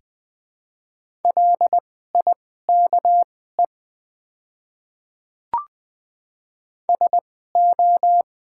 Morse code, heard as a keyed tone, about 20 wpm, spaced as for 14 wpm, 690 Hz. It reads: LIKE SO
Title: 2x Character Spacing - Top 100 Words - 20wpm